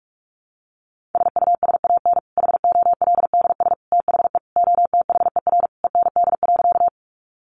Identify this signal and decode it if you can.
Morse code, heard as a keyed tone, about 44 wpm, 700 Hz. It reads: SUSAN HOLDS THE OTHER END.
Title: Easy Phrases 45wpm